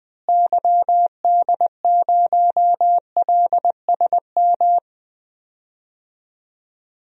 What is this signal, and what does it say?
Morse code, heard as a keyed tone, about 20 wpm, 700 Hz. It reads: YD0LSM
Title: DXCC-100 Call Signs - 6 Characters - Repeated 3x - 20wpm